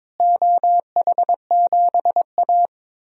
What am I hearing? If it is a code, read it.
Morse code, heard as a keyed tone, about 22 wpm, 700 Hz. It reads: OH7A
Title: DXCC-100 Call Signs - 4 Characters - Repeated 2x - 22wpm